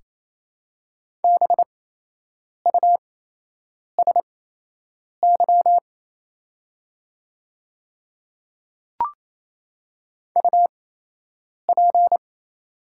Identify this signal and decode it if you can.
Morse code, heard as a keyed tone, about 28 wpm, spaced as for 8 wpm, 690 Hz. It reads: BUSY UP
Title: 8x Character Spacing - Top 500 Words - 28wpm